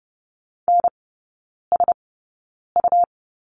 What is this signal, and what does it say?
Morse code, heard as a keyed tone, about 30 wpm, spaced as for 9 wpm, 690 Hz. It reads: NSU
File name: Lesson 17e - Letter u - Three Characters 30wpm